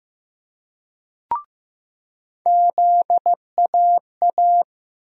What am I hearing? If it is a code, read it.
Morse code, heard as a keyed tone, about 15 wpm, 700 Hz. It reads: ZAA
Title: USA Call Sign Postfixes - 3 Letters 15wpm